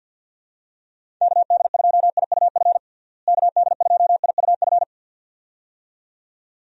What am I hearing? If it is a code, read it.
Morse code, heard as a keyed tone, about 51 wpm, 700 Hz. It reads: KD2IUF KD2IUF
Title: USA Call Signs - 6 Characters - Repeated 2x - 50wpm